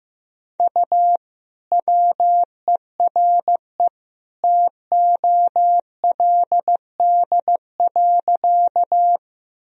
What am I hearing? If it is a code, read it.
Morse code, heard as a keyed tone, about 15 wpm, 700 Hz. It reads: U WERE TOLD.
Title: Sentences from Top 400 Words and Abbreviations 15wpm